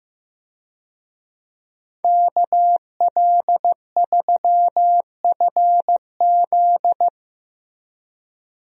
Morse code, heard as a keyed tone, about 15 wpm, 700 Hz. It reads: KL3FZ
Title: USA Call Signs - 5 Characters - Repeated 3x - 15wpm